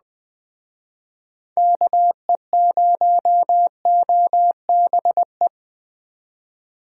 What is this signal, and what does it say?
Morse code, heard as a keyed tone, about 20 wpm, 700 Hz. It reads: KE0OBE
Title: Speed Racing - USA Call Signs 17wpm